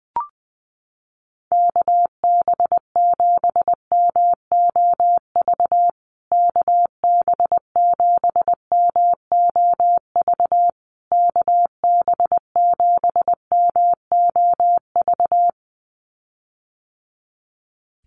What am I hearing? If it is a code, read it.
Morse code, heard as a keyed tone, about 20 wpm, 700 Hz. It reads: KB7MOV KB7MOV KB7MOV
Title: USA Call Signs Repeated 3x 20wpm